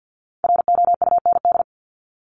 Morse code, heard as a keyed tone, about 50 wpm, 700 Hz. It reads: ROUND
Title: Top 200 Words Repeated 50wpm